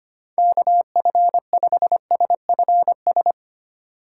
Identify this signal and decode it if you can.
Morse code, heard as a keyed tone, about 31 wpm, 700 Hz. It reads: KF5SFS
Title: Speed Racing - USA 2x3 Call Signs 25wpm